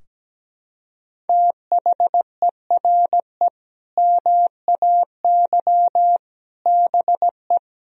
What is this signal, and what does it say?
Morse code, heard as a keyed tone, about 17 wpm, 700 Hz. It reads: THERE MAY BE
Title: Sentences from Top 1000 Words - Review Part 3 - 17wpm